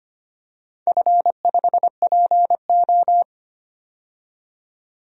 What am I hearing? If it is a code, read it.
Morse code, heard as a keyed tone, about 25 wpm, 700 Hz. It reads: F5PO
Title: DXCC-100 Call Signs - 4 Characters - 25wpm